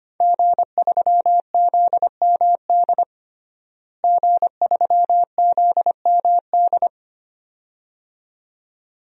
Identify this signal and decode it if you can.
Morse code, heard as a keyed tone, about 25 wpm, 700 Hz. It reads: G3ZMD G3ZMD
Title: DXCC-100 Call Signs - 5 Characters - Repeated 2x - 25wpm